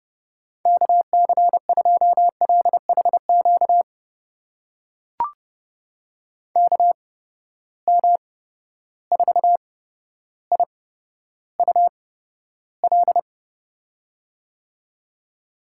Morse code, heard as a keyed tone, about 30 wpm, spaced as for 13 wpm, 700 Hz. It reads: KC2LHQ KM4IUL
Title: USA Call Signs - 6 Characters - 8x Character Spacing - 30wpm